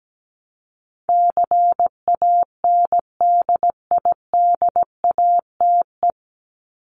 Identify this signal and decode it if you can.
Morse code, heard as a keyed tone, about 17 wpm, 700 Hz. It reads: CANDIDATE
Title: Top 1501-1750 Words 17wpm